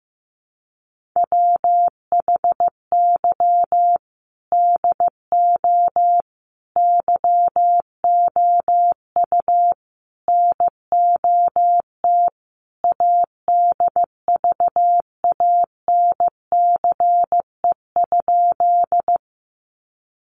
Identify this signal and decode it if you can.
Morse code, heard as a keyed tone, about 15 wpm, 700 Hz. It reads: WHY DO YOU NOT ADVANCE?
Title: Sentences from Top 1400 Words 15wpm